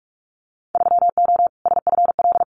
Morse code, heard as a keyed tone, about 44 wpm, 700 Hz. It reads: 3O SFL
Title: Winter Field Day Exchanges 45wpm